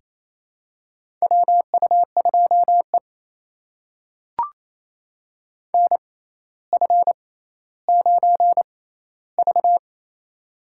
Morse code, heard as a keyed tone, about 28 wpm, spaced as for 14 wpm, 700 Hz. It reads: WU2E NF9V
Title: USA Call Signs - 4 Characters - 6x Character Spacing - 28wpm